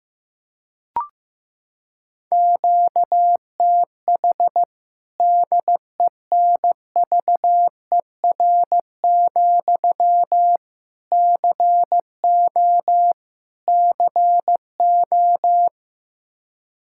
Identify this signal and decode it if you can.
Morse code, heard as a keyed tone, about 15 wpm, 700 Hz. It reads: QTH DENVER, CO CO
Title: Practice QSO - Exchange 1 - Beginning - 15wpm